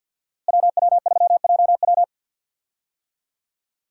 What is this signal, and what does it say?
Morse code, heard as a keyed tone, about 52 wpm, 700 Hz. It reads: WW3JW
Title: USA Call Signs - 5 Characters - Repeated 3x - 50wpm